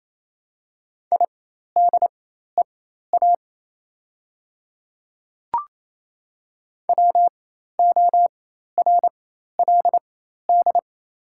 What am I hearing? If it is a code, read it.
Morse code, heard as a keyed tone, about 28 wpm, spaced as for 13 wpm, 690 Hz. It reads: IDEA WORLD
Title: 4x Character Spacing - Top 500 Words - 28wpm